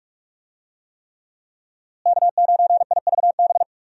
Morse code, heard as a keyed tone, about 46 wpm, 700 Hz. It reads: K9IVB
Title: USA Call Signs - 5 Characters - Repeated 3x - 45wpm